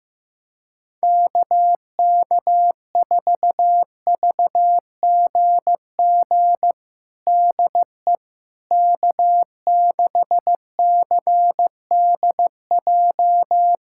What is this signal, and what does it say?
Morse code, heard as a keyed tone, about 16 wpm, 700 Hz. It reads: KK4VGG DE K6CDJ
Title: USA Callsign DE Callsign 15wpm